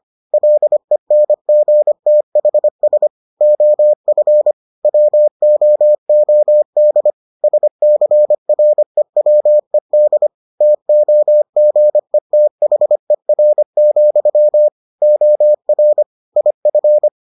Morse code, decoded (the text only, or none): LENGTHS OF WOOD SCREWED TOGETHER, OR IF